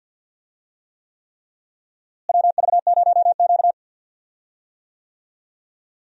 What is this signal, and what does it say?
Morse code, heard as a keyed tone, about 47 wpm, 700 Hz. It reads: WV0Q
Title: USA Call Signs - 4 Characters - Repeated 3x - 50wpm